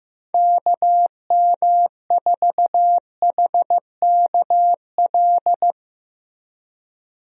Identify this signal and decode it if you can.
Morse code, heard as a keyed tone, about 15 wpm, 700 Hz. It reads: KM4HKL
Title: USA Call Signs - 6 Characters - Repeated 2x - 15wpm